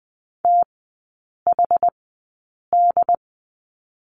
Morse code, heard as a keyed tone, about 20 wpm, spaced as for 8 wpm, 700 Hz. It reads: THD
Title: 184 Lesson 30e - Letter b - Three Characters 20wpm